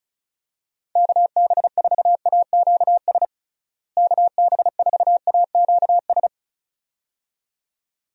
Morse code, heard as a keyed tone, about 34 wpm, 700 Hz. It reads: KB4AQS KB4AQS
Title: USA Call Signs - 6 Characters - Repeated 2x - 35wpm